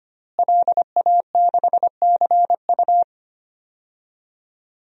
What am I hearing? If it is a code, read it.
Morse code, heard as a keyed tone, about 25 wpm, 700 Hz. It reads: LA6CU
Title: DXCC-100 Call Signs - 5 Characters - Repeated 3x - 25wpm